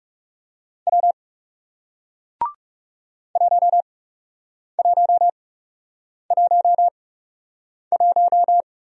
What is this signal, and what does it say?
Morse code, heard as a keyed tone, about 39 wpm, spaced as for 8 wpm, 700 Hz. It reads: W 1111